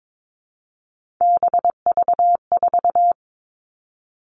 Morse code, heard as a keyed tone, about 22 wpm, 700 Hz. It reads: BV4
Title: All Cut Numbers - 3-Digit 22wpm